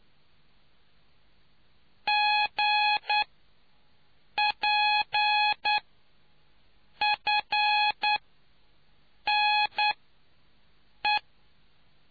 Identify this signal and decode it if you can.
Morse code, heard as a keyed tone, about 9 wpm, spaced as for 5 wpm, 840 Hz. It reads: GPFNE